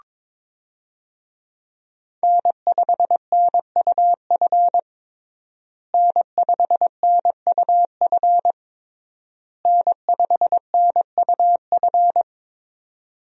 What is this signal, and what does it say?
Morse code, heard as a keyed tone, about 22 wpm, 700 Hz. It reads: N5NUF N5NUF N5NUF
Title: USA Call Signs - 5 Characters - Repeated 3x - 22wpm